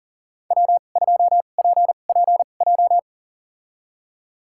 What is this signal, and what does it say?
Morse code, heard as a keyed tone, about 37 wpm, spaced as for 28 wpm, 700 Hz. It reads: W2PPJ
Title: USA Call Signs - 5 Characters - 2x Character Spacing - 40wpm